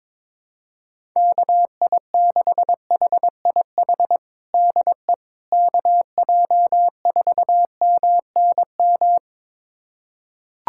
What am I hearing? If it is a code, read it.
Morse code, heard as a keyed tone, about 22 wpm, 700 Hz. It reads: KI6HIH DE KJ4MNM
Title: USA Callsign DE Callsign 22wpm